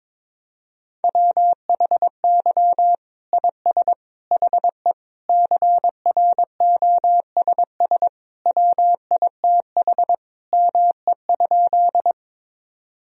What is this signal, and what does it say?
Morse code, heard as a keyed tone, about 22 wpm, 700 Hz. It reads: WHY IS HE CROSS WITH ME?
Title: Sentences from Top 1000 Words - Review Part 1 - 22wpm